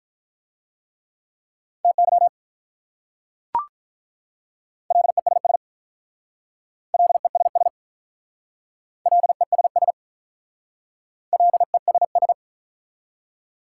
Morse code, heard as a keyed tone, about 44 wpm, 700 Hz. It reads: TY LESS LESS LESS LESS